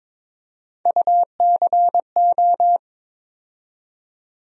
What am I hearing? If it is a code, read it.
Morse code, heard as a keyed tone, about 21 wpm, 700 Hz. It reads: UCO
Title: USA Call Sign Postfixes - 3 Letters 22wpm